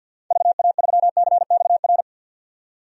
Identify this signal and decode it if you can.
Morse code, heard as a keyed tone, about 48 wpm, 700 Hz. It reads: VA3CXR
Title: DXCC-100 Call Signs - 6 Characters - 50wpm